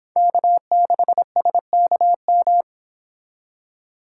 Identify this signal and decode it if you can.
Morse code, heard as a keyed tone, about 28 wpm, 700 Hz. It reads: K6SKM